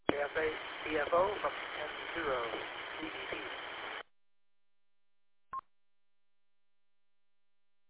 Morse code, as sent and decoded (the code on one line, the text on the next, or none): none